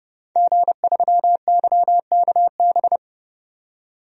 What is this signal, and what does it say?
Morse code, heard as a keyed tone, about 30 wpm, 700 Hz. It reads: G3YKB
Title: DXCC-100 Call Signs - 5 Characters - 30wpm